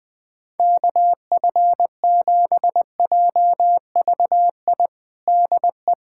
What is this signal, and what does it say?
Morse code, heard as a keyed tone, about 20 wpm, 700 Hz. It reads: KF7JVI DE
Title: USA Callsign DE Callsign - 6 Characters - 20wpm